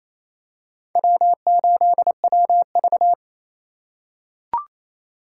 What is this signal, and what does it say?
Morse code, heard as a keyed tone, about 28 wpm, 700 Hz. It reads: W8WV
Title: USA Call Signs - 4 Characters - 2x Character Spacing - 28wpm